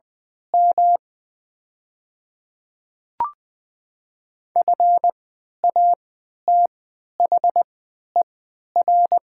Morse code, heard as a keyed tone, about 20 wpm, spaced as for 11 wpm, 700 Hz. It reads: M FATHER
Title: 3x Character Spacing - Top 300 Words - 20wpm